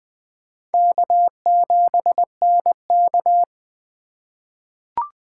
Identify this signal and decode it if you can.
Morse code, heard as a keyed tone, about 20 wpm, 700 Hz. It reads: K7NK
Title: USA Call Signs - 4 Characters - 4x Character Spacing - 20wpm